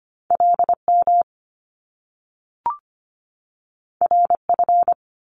Morse code, heard as a keyed tone, about 25 wpm, 700 Hz. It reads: LM RF
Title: Top Two Letter Combinations in Words 25wpm